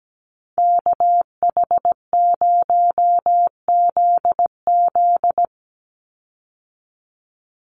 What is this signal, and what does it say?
Morse code, heard as a keyed tone, about 17 wpm, 700 Hz. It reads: KH0ZZ
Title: Call Signs Repeated 2x - 17wpm